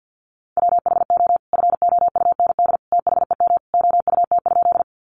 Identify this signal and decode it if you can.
Morse code, heard as a keyed tone, about 50 wpm, 700 Hz. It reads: WHO FOUND THEM OUT?
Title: Sentences from Top 300 Words 50wpm